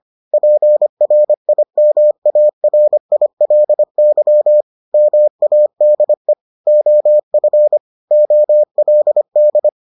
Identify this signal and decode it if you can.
Morse code, heard as a keyed tone, about 25 wpm, 600 Hz. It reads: PRIMARILY MADE OF OLD